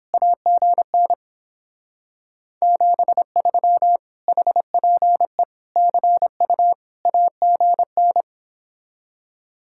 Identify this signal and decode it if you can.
Morse code, heard as a keyed tone, about 29 wpm, 700 Hz. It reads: AGN 73 HPE CU AGN